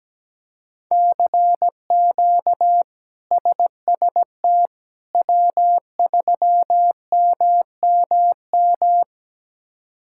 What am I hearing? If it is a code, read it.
Morse code, heard as a keyed tone, about 17 wpm, 700 Hz. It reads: CQ SST W3MMM
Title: K1USN SST QSO Practice - 17wpm